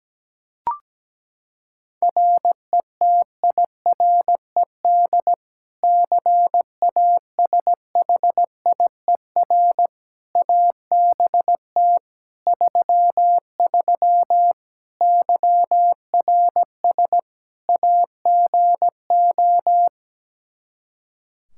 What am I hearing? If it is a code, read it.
Morse code, heard as a keyed tone, about 17 wpm, 700 Hz. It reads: RETIRED CASHIER ABT 33 YRS AGO
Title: Practice QSO - Exchange 3 - More Info - 17wpm